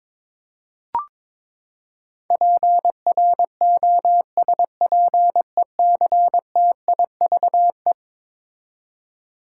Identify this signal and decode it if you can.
Morse code, heard as a keyed tone, about 22 wpm, 700 Hz. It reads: PROSPECTIVE
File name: practice / 22wpm / Eleven Letter Words - 22wpm